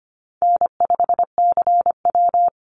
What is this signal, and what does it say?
Morse code, heard as a keyed tone, about 25 wpm, 700 Hz. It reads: N5CW
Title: Call Signs Repeated 3x - 25wpm